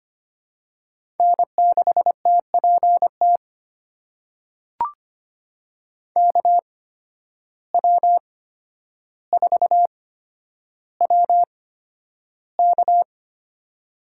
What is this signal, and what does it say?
Morse code, heard as a keyed tone, about 25 wpm, spaced as for 11 wpm, 700 Hz. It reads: N6TPT KW4WK
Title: USA Call Signs - 5 Characters - 8x Character Spacing - 25wpm